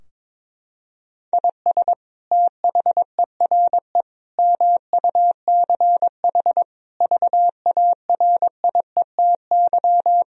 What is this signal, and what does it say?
Morse code, heard as a keyed tone, about 22 wpm, 700 Hz. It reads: IS THERE MUCH VARIETY
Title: Sentences from Top 1000 Words - Review Part 4 - 22wpm